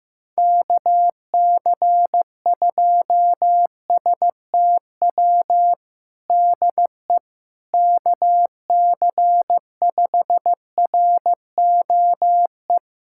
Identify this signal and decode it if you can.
Morse code, heard as a keyed tone, about 16 wpm, 700 Hz. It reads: KC2STW DE KC5ROE
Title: USA Callsign DE Callsign - 6 Characters - 15wpm